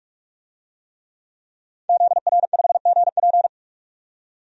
Morse code, heard as a keyed tone, about 44 wpm, 690 Hz. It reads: ZR5GP
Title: DXCC-100 Call Signs - 5 Characters - 45wpm